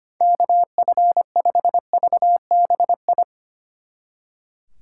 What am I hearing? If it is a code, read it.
Morse code, heard as a keyed tone, about 28 wpm, 700 Hz. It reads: KF5VBI
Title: Speed Racing - USA 2x3 Call Signs 25wpm